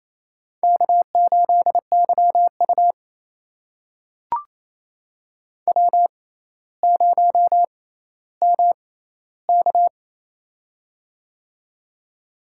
Morse code, heard as a keyed tone, about 28 wpm, spaced as for 14 wpm, 700 Hz. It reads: K8YU W0MK
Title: USA Call Signs - 4 Characters - 6x Character Spacing - 28wpm